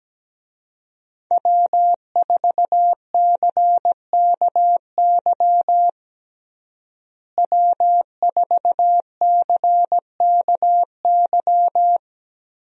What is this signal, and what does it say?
Morse code, heard as a keyed tone, about 17 wpm, 700 Hz. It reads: W4CKY W4CKY
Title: USA Call Signs - 5 Characters - Repeated 3x - 17wpm